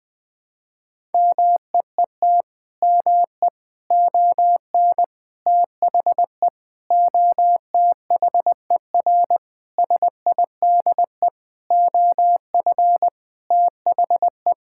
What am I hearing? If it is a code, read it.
Morse code, heard as a keyed tone, about 20 wpm, 700 Hz. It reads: MEET ME ON THE OTHER SIDE OF THE
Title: Sentences with 6-Word Phrases - 20wpm